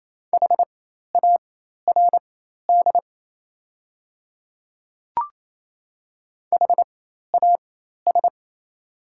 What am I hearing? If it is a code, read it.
Morse code, heard as a keyed tone, about 28 wpm, spaced as for 13 wpm, 700 Hz. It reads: HARD HAS
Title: 4x Character Spacing - Top 300 Words - 28wpm